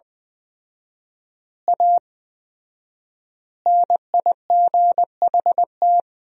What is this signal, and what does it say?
Morse code, heard as a keyed tone, about 21 wpm, 700 Hz. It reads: A NIGHT
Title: 4x Word-Spacing - Sentences from Top 300 Words - 20wpm